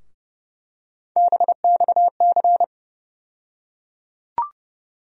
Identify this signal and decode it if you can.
Morse code, heard as a keyed tone, about 30 wpm, 700 Hz. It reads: BXC
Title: USA Call Sign Postfixes - 3 Letters 30wpm